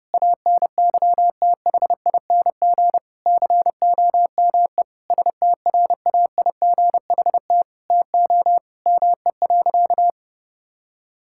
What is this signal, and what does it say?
Morse code, heard as a keyed tone, about 28 wpm, 700 Hz. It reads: ANYTHING COME STRAIGHT TO ME.
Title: Sentences from Top 1000 Words - Review Part 6 - 30wpm